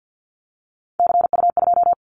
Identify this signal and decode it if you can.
Morse code, heard as a keyed tone, about 50 wpm, 700 Hz. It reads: CU2
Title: Callsign Prefixes for Top 50 DXCC - 50wpm